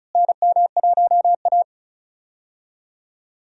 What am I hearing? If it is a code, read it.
Morse code, heard as a keyed tone, about 36 wpm, 700 Hz. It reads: NM1A
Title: USA Call Signs - 4 Characters - Repeated 2x - 35wpm